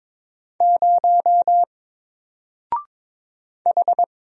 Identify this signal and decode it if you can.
Morse code, heard as a keyed tone, about 22 wpm, 700 Hz. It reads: 0 H
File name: Single-Letter-Number-22wpm